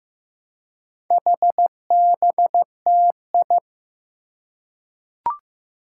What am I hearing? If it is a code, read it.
Morse code, heard as a keyed tone, about 15 wpm, 700 Hz. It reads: HBTI
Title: Four Challenging Letter-Number Combinations 15wpm